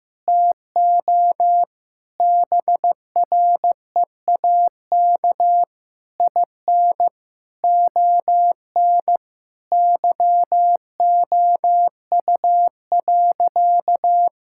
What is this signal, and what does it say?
Morse code, heard as a keyed tone, about 16 wpm, 700 Hz. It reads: TO BREAK IN ON YOU.
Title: Sentences from Top 1000 Words - Review Part 6 - 15wpm